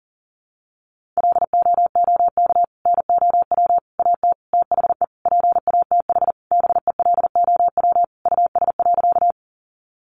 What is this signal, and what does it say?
Morse code, heard as a keyed tone, about 40 wpm, 700 Hz. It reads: LOOK NOW AT THE PATH BELOW US.
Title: Sentences from Top 1400 Words 40wpm